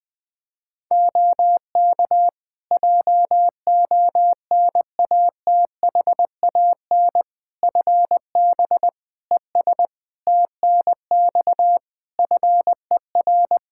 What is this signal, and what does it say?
Morse code, heard as a keyed tone, about 20 wpm, 700 Hz. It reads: OK JONATHAN FB ES TNX FER
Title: Practice QSO - Exchange 2 - Info - 20wpm